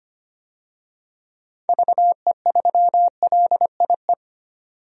Morse code, heard as a keyed tone, about 25 wpm, 700 Hz. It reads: VE3LIE
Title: DXCC-100 Call Signs - 6 Characters - 25wpm